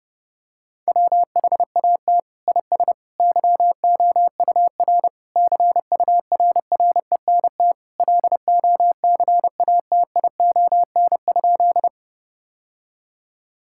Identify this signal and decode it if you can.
Morse code, heard as a keyed tone, about 30 wpm, 700 Hz. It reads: WHAT IS YOUR CURRENT LOCATION?
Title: Sentences from Top 1000 Words - Review Part 6 - 30wpm